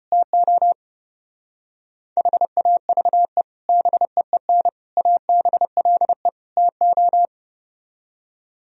Morse code, heard as a keyed tone, about 33 wpm, 700 Hz. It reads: TO HAVE BEEN ABLE TO